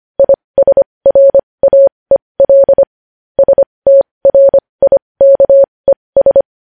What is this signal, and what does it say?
Morse code, heard as a keyed tone, about 25 wpm, spaced as for 20 wpm, 550 Hz. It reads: ISRAEL STRIKES
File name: podcast / News headlines-1x-25x20